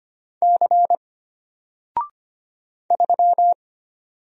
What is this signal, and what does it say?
Morse code, heard as a keyed tone, about 25 wpm, 700 Hz. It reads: C 3
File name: Single-Letter-Number-25wpm